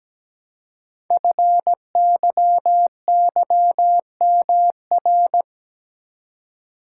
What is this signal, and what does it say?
Morse code, heard as a keyed tone, about 17 wpm, 700 Hz. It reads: FYYMR
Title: Code Groups - 5 Letter:Numbers - 17wpm